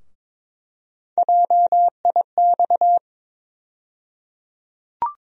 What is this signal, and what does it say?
Morse code, heard as a keyed tone, about 21 wpm, 700 Hz. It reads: JIX
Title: USA Call Sign Postfixes - 3 Letters 22wpm